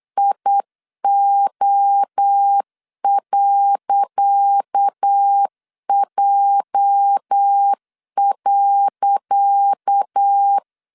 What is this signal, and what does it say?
Morse code, heard as a keyed tone, about 8 wpm, 800 Hz. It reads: IO.J.
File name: LEZIONI / 10 SIGNS / 10C DOT 8wpm 3649